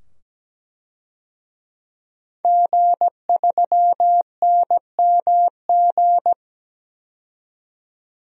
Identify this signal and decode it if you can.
Morse code, heard as a keyed tone, about 17 wpm, 700 Hz. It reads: G3NMG
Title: DXCC-100 Call Signs - 5 Characters - 17wpm